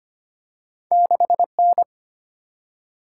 Morse code, heard as a keyed tone, about 25 wpm, 700 Hz. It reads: 6N
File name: Two Letter-Number Combinations - 25wpm